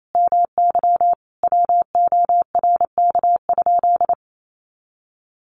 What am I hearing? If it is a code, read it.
Morse code, heard as a keyed tone, about 28 wpm, 700 Hz. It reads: MY WORK?
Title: Sentences from Top 500 Words 28wpm